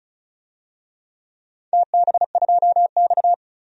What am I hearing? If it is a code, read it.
Morse code, heard as a keyed tone, about 36 wpm, 700 Hz. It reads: TB2X
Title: DXCC-100 Call Signs - 4 Characters - 35wpm